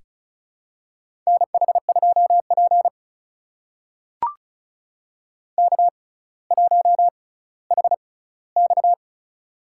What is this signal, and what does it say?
Morse code, heard as a keyed tone, about 36 wpm, spaced as for 18 wpm, 700 Hz. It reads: NH2P K1HX
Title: USA Call Signs - 4 Characters - 6x Character Spacing - 35wpm